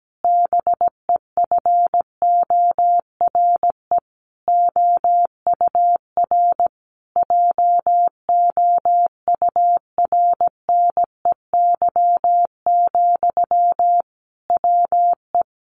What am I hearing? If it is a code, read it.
Morse code, heard as a keyed tone, about 17 wpm, 700 Hz. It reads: BEFORE OUR JOURNEY, WE
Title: A Sound Of Thunder 17wpm